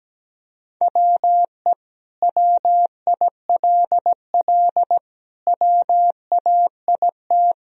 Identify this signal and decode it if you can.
Morse code, heard as a keyed tone, about 17 wpm, 700 Hz. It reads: WE WILL WAIT
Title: Sentences from Top 1000 Words - Review Part 1 - 17wpm